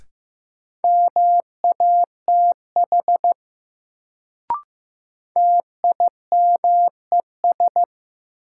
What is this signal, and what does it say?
Morse code, heard as a keyed tone, about 15 wpm, 700 Hz. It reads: MATH TIMES
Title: Sentences from Top 1000 Words - Review Part 6 - New Word Drill - 15wpm